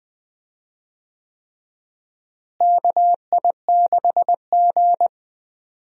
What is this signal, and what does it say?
Morse code, heard as a keyed tone, about 20 wpm, 690 Hz. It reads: KI6G
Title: USA Call Signs - 4 Characters - Repeated 3x - 20wpm